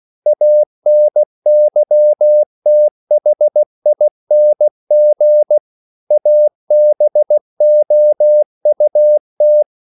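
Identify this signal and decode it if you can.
Morse code, heard as a keyed tone, about 16 wpm, 600 Hz. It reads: ANYTHING ABOUT
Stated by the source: English quotations sent in CW as practice at 16 words per minute.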